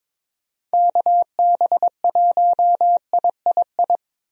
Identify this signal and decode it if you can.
Morse code, heard as a keyed tone, about 22 wpm, 700 Hz. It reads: KB1III
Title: USA Call Signs - 6 Characters - Repeated 3x - 22wpm